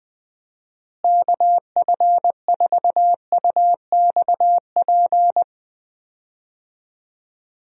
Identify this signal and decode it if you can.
Morse code, heard as a keyed tone, about 20 wpm, 700 Hz. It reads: KF4UXP
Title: USA Call Signs - 6 Characters - Repeated 3x - 20wpm